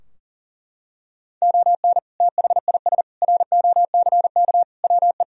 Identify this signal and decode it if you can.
Morse code, heard as a keyed tone, about 40 wpm, 700 Hz. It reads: ON THIS ROCK WE
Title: Sentences from Top 1000 Words - Review Part 2 - 40wpm